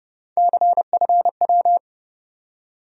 Morse code, heard as a keyed tone, about 30 wpm, 700 Hz. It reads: CFW
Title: USA Call Sign Postfixes - 3 Letters 30wpm